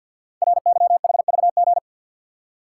Morse code, heard as a keyed tone, about 51 wpm, 700 Hz. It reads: RYHVG
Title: Code Groups - 5 Letters - 50wpm